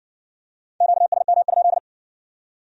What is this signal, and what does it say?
Morse code, heard as a keyed tone, about 55 wpm, 700 Hz. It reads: QSM?